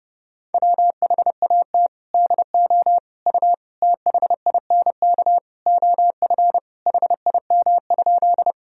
Morse code, heard as a keyed tone, about 32 wpm, 700 Hz. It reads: WHAT DO U THINK OF HIM?
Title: Sentences from Top 200 Words and Abbreviations 30wpm